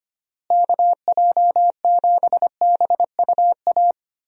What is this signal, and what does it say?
Morse code, heard as a keyed tone, about 25 wpm, 700 Hz. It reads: KJ7BUA
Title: USA Call Signs - 6 Characters - Repeated 3x - 25wpm